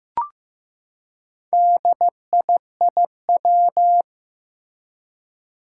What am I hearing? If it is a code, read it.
Morse code, heard as a keyed tone, about 15 wpm, 700 Hz. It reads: DIIW
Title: Four Challenging Letter-Number Combinations 15wpm